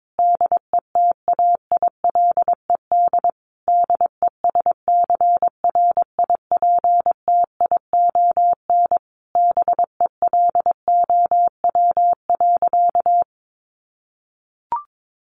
Morse code, heard as a keyed tone, about 22 wpm, 700 Hz. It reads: DETAILED DESCRIPTION BELOW.
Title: Sentences from Top 1500 Words 22wpm